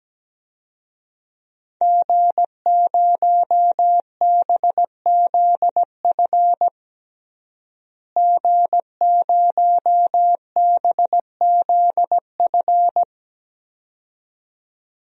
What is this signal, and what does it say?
Morse code, heard as a keyed tone, about 17 wpm, 700 Hz. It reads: G0BZF G0BZF
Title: DXCC-100 Call Signs - 5 Characters - Repeated 2x - 17wpm